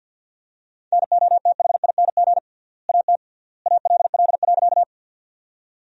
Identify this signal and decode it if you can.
Morse code, heard as a keyed tone, about 49 wpm, 700 Hz. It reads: NOTHING AT ALL.
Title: 3x Word-Spacing - Sentences from Top 500 Words - 50wpm